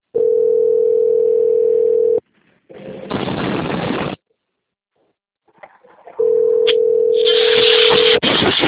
Ringback tone (North America), 440 + 480 Hz.
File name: out- - - - - .16105